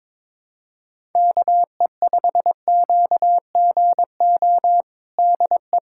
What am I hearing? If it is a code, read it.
Morse code, heard as a keyed tone, about 22 wpm, 700 Hz. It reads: KE5QGO DE
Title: USA Callsign DE Callsign - 6 Characters - 22wpm